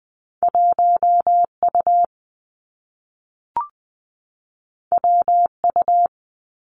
Morse code, heard as a keyed tone, about 20 wpm, 700 Hz. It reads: 1 U W U
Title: Two Challenging Letter-Number Combinations 20wpm